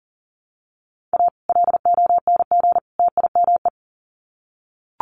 Morse code, heard as a keyed tone, about 40 wpm, 700 Hz. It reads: A LONG TIME
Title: Sets of 3 Words 40wpm